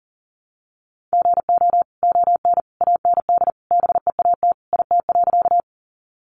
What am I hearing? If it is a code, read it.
Morse code, heard as a keyed tone, about 40 wpm, 700 Hz. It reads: GO ON AND BEAT IT.
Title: Sentences from Top 1300 Words 40wpm